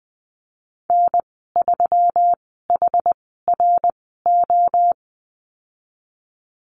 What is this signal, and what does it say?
Morse code, heard as a keyed tone, about 20 wpm, spaced as for 15 wpm, 700 Hz. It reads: N3HRO
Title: 173 Lesson 28g - Number 9 - Call Signs 20wpm